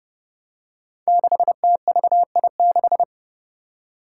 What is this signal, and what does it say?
Morse code, heard as a keyed tone, about 30 wpm, 700 Hz. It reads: 6TVI6
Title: Five Challenging Letter-Number Combinations 30wpm